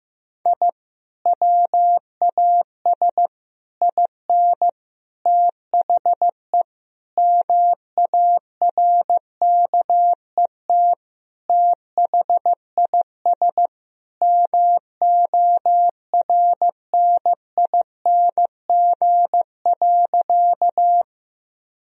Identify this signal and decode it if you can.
Morse code, heard as a keyed tone, about 16 wpm, 700 Hz. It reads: I WAS IN THE MARKET THIS MORNING.
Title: Sentences from Top 1000 Words - Review Part 3 - 15wpm